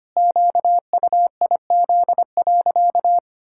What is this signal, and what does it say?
Morse code, heard as a keyed tone, about 25 wpm, 700 Hz. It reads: QUIZ.
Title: Pangram Challenge v2 - 25wpm